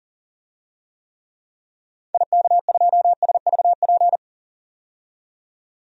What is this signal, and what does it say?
Morse code, heard as a keyed tone, about 40 wpm, 700 Hz. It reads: IK2SVP
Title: DXCC-100 Call Signs - 6 Characters - 40wpm